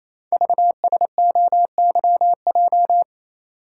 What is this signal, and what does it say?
Morse code, heard as a keyed tone, about 27 wpm, 700 Hz. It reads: VSOYJ
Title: Code Groups - 5 Letters - 28wpm